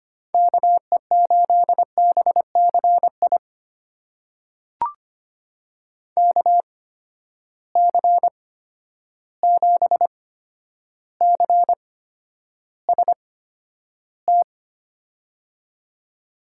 Morse code, heard as a keyed tone, about 25 wpm, spaced as for 11 wpm, 700 Hz. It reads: KE8BCI KC7CST
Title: USA Call Signs - 6 Characters - 8x Character Spacing - 25wpm